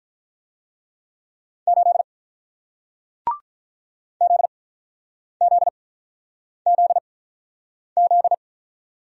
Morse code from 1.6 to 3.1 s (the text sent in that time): 8